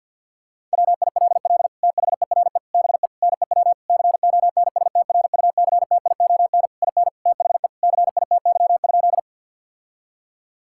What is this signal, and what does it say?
Morse code, heard as a keyed tone, about 50 wpm, 700 Hz. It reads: WILL THERE BE NEW CONSTRUCTION IN THE CITY?
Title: Sentences from Top 1600 Words 50wpm